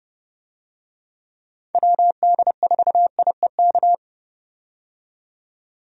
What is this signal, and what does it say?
Morse code, heard as a keyed tone, about 30 wpm, 700 Hz. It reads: WD4IEK
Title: USA Call Signs - 6 Characters - 30wpm